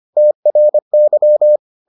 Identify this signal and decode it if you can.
Morse code, heard as a keyed tone, about 25 wpm, 600 Hz. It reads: TRY